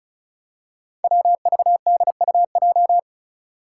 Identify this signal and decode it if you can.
Morse code, heard as a keyed tone, about 36 wpm, 700 Hz. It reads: WVDUJ
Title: Five Challenging Letter-Number Combinations 35wpm